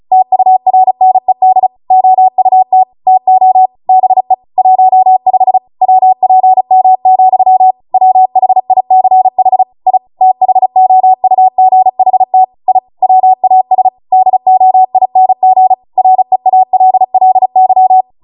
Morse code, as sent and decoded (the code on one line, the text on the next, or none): - ..- .-. -. -.. / --- ..- - / - --- / -... / .---- ..... / .-- .--. -- --..-- / .-- .... .. -.-. .... / .. / - .... --- ..- --. .... - / .. / .-- .- ... / -.. --- .. -. --. / .-. .- .-.. .-.. -.--
TURNED OUT TO BE 15 WPM, WHICH I THOUGHT I WAS DOING REALLY